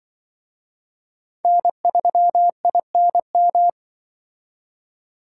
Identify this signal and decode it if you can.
Morse code, heard as a keyed tone, about 24 wpm, 700 Hz. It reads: N3INM
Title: Speed Racing - USA 1x3 Call Signs 20wpm